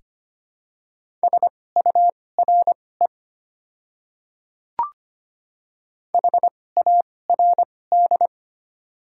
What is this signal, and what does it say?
Morse code, heard as a keyed tone, about 25 wpm, spaced as for 18 wpm, 700 Hz. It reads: SURE HARD
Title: 2x Character Spacing - Top 300 Words - 25wpm